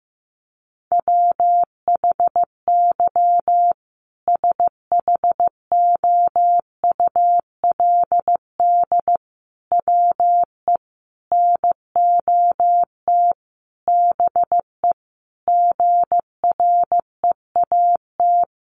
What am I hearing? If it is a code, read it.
Morse code, heard as a keyed tone, about 15 wpm, 700 Hz. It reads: WHY SHOULD WE NOT BE GREAT
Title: Sentences from Top 1500 Words 15wpm